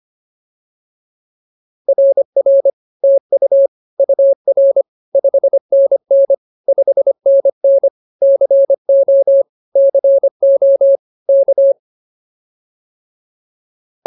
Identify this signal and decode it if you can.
Morse code, heard as a keyed tone, about 24 wpm, 550 Hz. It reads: RR TU UR 5NN 5NN CO CO K